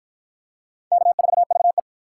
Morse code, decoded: K4VE

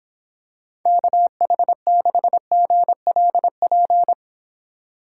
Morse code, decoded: KH6GLP

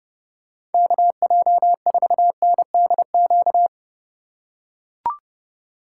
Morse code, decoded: KJ4NDQ

